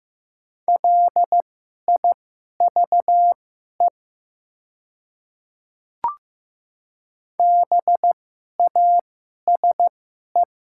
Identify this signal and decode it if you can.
Morse code, heard as a keyed tone, about 15 wpm, spaced as for 11 wpm, 700 Hz. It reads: LIVE BASE
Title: 2x Character Spacing - Top 300 Words - 15wpm